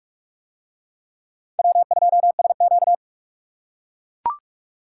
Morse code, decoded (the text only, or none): W2SQ